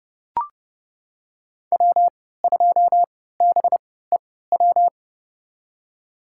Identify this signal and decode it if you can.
Morse code, heard as a keyed tone, about 32 wpm, spaced as for 17 wpm, 700 Hz. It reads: W2BEW
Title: USA Call Signs - 5 Characters - 3x Character Spacing - 30wpm